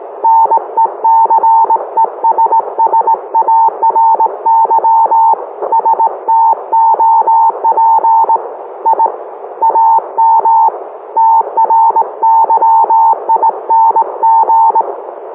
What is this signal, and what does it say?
Morse code, heard as a keyed tone, about 19 wpm, 900 Hz. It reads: NECESSARY STOP I AM TRYING